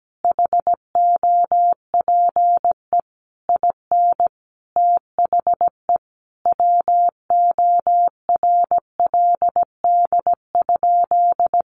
Morse code, heard as a keyed tone, about 17 wpm, 700 Hz. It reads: HOPE IN THE WORLD?
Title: Sentences from Top 1400 Words 17wpm